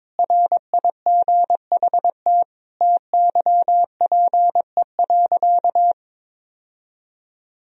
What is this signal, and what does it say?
Morse code, heard as a keyed tone, about 22 wpm, 700 Hz. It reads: RIGHT TYPE.
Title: Sentences from Top 1000 Words - Review Part 3 - 22wpm